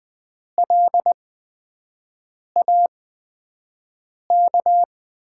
Morse code, heard as a keyed tone, about 20 wpm, spaced as for 5 wpm, 700 Hz. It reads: LAK
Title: USA Call Signs - 5 Characters - 8x Character Spacing - 20wpm